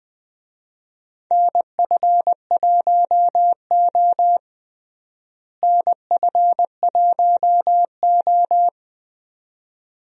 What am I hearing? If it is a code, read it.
Morse code, heard as a keyed tone, about 20 wpm, 700 Hz. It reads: NF1O NF1O
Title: USA Call Signs - 4 Characters - Repeated 2x - 20wpm